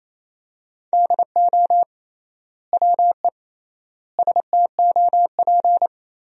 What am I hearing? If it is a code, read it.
Morse code, heard as a keyed tone, about 28 wpm, 700 Hz. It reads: DO WE STOP
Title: 3x Word-Spacing - Sentences from Top 300 Words - 28wpm